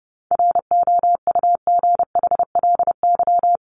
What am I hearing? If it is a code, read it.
Morse code, heard as a keyed tone, about 30 wpm, 700 Hz. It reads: ROUGHLY
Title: Top 1251-1500 Words 30wpm